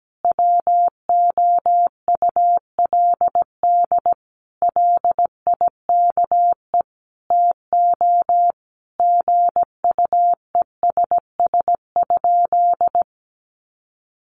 Morse code, heard as a keyed tone, about 17 wpm, 700 Hz. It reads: WOULD LIKE TO GUESS?
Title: Sentences from Top 1500 Words 17wpm